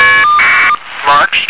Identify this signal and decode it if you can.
Caller ID data burst, with speech.